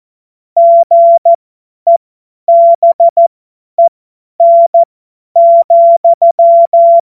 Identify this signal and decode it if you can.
Morse code, heard as a keyed tone, about 14 wpm, spaced as for 10 wpm, 680 Hz. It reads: GEBEN,
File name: Morsekurs / Witze / Witz 30 Januar 2024 50 F70